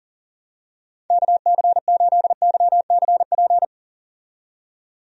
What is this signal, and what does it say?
Morse code, heard as a keyed tone, about 40 wpm, 700 Hz. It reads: KC8YCP
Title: USA Call Signs - 6 Characters - 40wpm